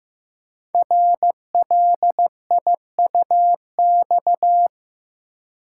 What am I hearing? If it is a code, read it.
Morse code, heard as a keyed tone, about 15 wpm, 700 Hz. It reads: RLIUX
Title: Code Groups - 5 Letter:Numbers - 15wpm